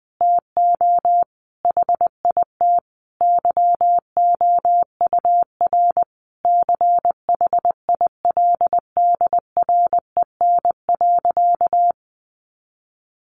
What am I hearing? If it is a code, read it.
Morse code, heard as a keyed tone, about 20 wpm, 700 Hz. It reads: TO HIT YOUR CHILDREN.
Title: Sentences from Top 1200 Words 20wpm